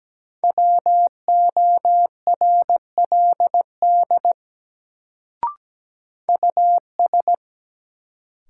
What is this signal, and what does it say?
Morse code, heard as a keyed tone, about 17 wpm, 700 Hz. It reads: WORLD US
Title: Sets of 2 Words - New Word Drill - 17wpm